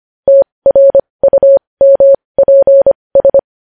Morse code, heard as a keyed tone, about 25 wpm, spaced as for 20 wpm, 550 Hz. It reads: TRUMPS
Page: news-headlines-in-morse-code-at-20-wpm